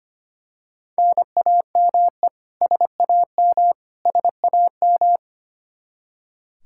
Morse code, heard as a keyed tone, about 25 wpm, 700 Hz. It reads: NAME SAM SAM
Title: Practice QSO - Exchange 1 - Beginning - 25wpm